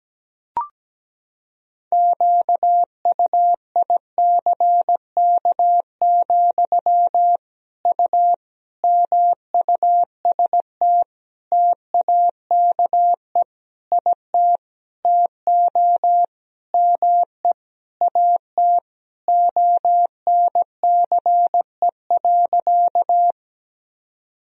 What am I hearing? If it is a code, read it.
Morse code, heard as a keyed tone, about 17 wpm, 700 Hz. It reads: QUICK, U MUST TAKE IT TO ME AT ONCE.
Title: Sentences from Top 500 Words and Abbreviations 17wpm